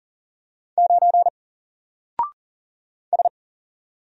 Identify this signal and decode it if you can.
Morse code, heard as a keyed tone, about 40 wpm, 700 Hz. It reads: 9 S
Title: Single Letter-Number Alt 40wpm